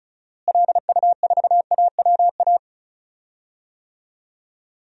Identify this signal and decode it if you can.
Morse code, heard as a keyed tone, about 34 wpm, 700 Hz. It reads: LU4AWA
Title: DXCC-100 Call Signs - 6 Characters - Repeated 3x - 35wpm